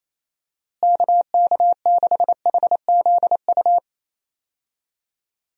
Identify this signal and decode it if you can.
Morse code, heard as a keyed tone, about 28 wpm, 700 Hz. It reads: KK6HZU
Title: USA Call Signs - 6 Characters - Repeated 3x - 28wpm